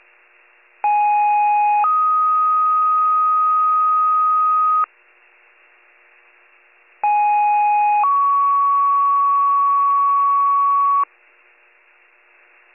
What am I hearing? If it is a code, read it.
Two-tone page, heard as a one-second tone, then a three-second tone, about 830 then 1250 Hz, about 830 then 1120 Hz. It is heard from 0.8 to 11.0 s.